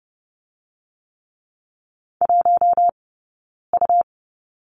Morse code, heard as a keyed tone, about 30 wpm, 700 Hz. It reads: 1 U
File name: course / 30:10wpm / 099 Lesson 17d - Letter u - Two Characters 30wpm